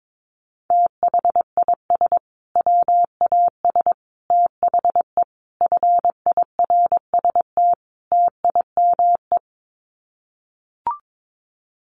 Morse code, heard as a keyed tone, about 22 wpm, 700 Hz. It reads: THIS WAS THE FIRST TIME
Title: Sets of 5 Words Encore 22wpm